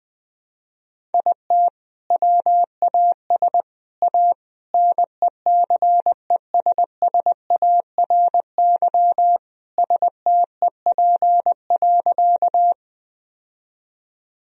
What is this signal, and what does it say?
Morse code, heard as a keyed tone, about 20 wpm, 700 Hz. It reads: IT WAS A NECESSARY STEP.
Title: Sentences from Top 1000 Words - Review Part 2 - 20wpm